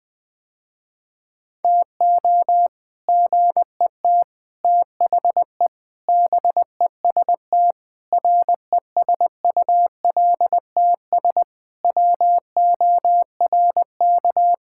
Morse code, heard as a keyed tone, about 20 wpm, 700 Hz. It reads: TO GET THE BEST RESULTS WORK
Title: Sentences with 3-Word Phrases - Spoken Beginning - 20wpm